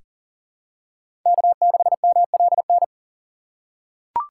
Morse code, heard as a keyed tone, about 37 wpm, 700 Hz. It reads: K6MLN
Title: USA Call Signs - 5 Characters - 4x Character Spacing - 40wpm